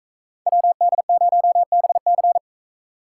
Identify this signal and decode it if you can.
Morse code, heard as a keyed tone, about 43 wpm, 700 Hz. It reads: WD0BC